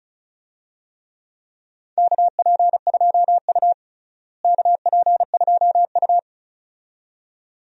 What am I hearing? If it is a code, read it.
Morse code, heard as a keyed tone, about 34 wpm, 700 Hz. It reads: KP2U KP2U
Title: DXCC-100 Call Signs - 4 Characters - Repeated 2x - 35wpm